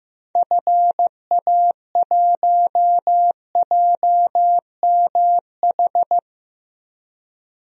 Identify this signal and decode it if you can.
Morse code, heard as a keyed tone, about 15 wpm, 700 Hz. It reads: FA1JMH
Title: DXCC-100 Call Signs - 6 Characters - 15wpm